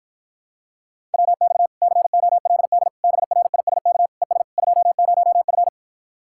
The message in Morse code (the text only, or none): WX COLD BRISK ES 20F